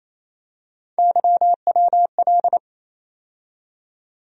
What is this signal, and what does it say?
Morse code, heard as a keyed tone, about 28 wpm, 700 Hz. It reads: YWL
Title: USA Call Sign Postfixes - 3 Letters 28wpm